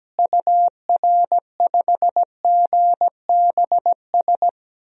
Morse code, heard as a keyed tone, about 17 wpm, 700 Hz. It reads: UR5GBS
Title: DXCC-100 Call Signs - 6 Characters - Repeated 2x - 17wpm